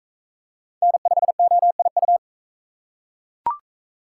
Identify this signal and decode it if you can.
Morse code, heard as a keyed tone, about 40 wpm, 700 Hz. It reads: N5OIU